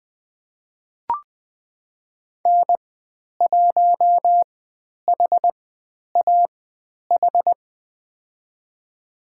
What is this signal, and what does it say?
Morse code, heard as a keyed tone, about 20 wpm, spaced as for 10 wpm, 700 Hz. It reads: N1HAH
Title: 109 Lesson 18g - Letter m - Call Signs 20wpm